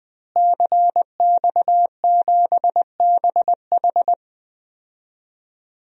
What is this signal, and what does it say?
Morse code, heard as a keyed tone, about 21 wpm, 700 Hz. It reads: CX7BH
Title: DXCC-100 Call Signs - 5 Characters - Repeated 2x - 20wpm